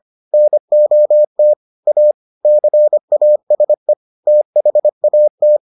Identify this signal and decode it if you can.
Morse code, heard as a keyed tone, about 25 wpm, 600 Hz. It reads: NOT A CASE THAT